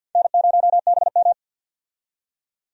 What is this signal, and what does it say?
Morse code, heard as a keyed tone, about 51 wpm, 700 Hz. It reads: N0BM
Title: USA Call Signs - 4 Characters - 50wpm